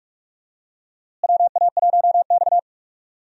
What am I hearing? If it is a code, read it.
Morse code, heard as a keyed tone, about 41 wpm, 700 Hz. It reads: WA1X
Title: USA Call Signs - 4 Characters - Repeated 2x - 45wpm